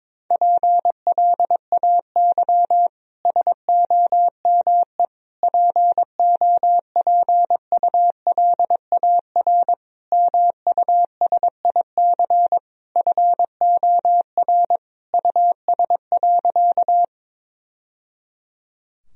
Morse code, heard as a keyed tone, about 22 wpm, 700 Hz. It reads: PLAY SOME POPULAR MUSIC FOR US.
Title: Sentences from Top 1000 Words - Review Part 3 - 22wpm